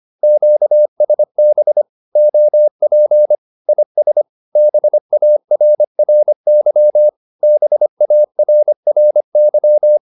Morse code, heard as a keyed tone, about 25 wpm, 600 Hz. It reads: QSB OP IS BARRY BARRY
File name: QSO201 25